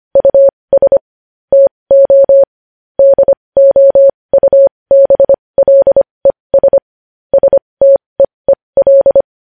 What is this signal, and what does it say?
Morse code, heard as a keyed tone, about 25 wpm, spaced as for 20 wpm, 550 Hz. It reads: US TO DOUBLES STEEL
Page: news-headlines-in-morse-code-at-20-wpm